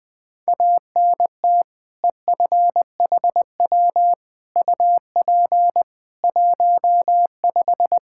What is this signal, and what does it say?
Morse code, heard as a keyed tone, about 20 wpm, 700 Hz. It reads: ANT EFHW UP 15
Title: Practice QSO - Exchange 2 - Info - 20wpm